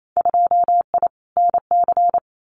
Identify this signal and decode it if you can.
Morse code, heard as a keyed tone, about 28 wpm, 700 Hz. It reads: 2I NC
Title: Winter Field Day Exchanges 28wpm